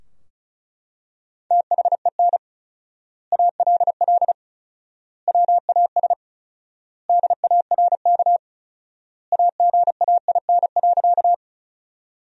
Morse code, heard as a keyed tone, about 36 wpm, 700 Hz. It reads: THEN ALL WAS DARK AGAIN.
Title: 4x Word-Spacing - Sentences from Top 500 Words - 35wpm